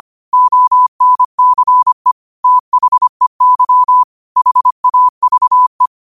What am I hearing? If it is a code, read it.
Morse code, heard as a keyed tone, about 25 wpm, 1000 Hz. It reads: ONCE THEY HAVE
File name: qotdfiles / 25wpm